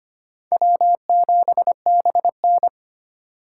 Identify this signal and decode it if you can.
Morse code, heard as a keyed tone, about 25 wpm, 700 Hz. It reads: W7BN
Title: USA Call Signs - 3x Character Spacing - 25wpm